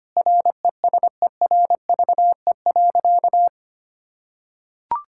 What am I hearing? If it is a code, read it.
Morse code, heard as a keyed tone, about 25 wpm, 700 Hz. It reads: RESERVE.
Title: Sentences from Top 1000 Words - Review Part 6 - 25wpm